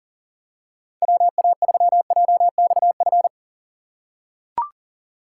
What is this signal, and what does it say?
Morse code, heard as a keyed tone, about 40 wpm, 690 Hz. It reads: WA3JXF
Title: USA Call Signs - 6 Characters - 2x Character Spacing - 40wpm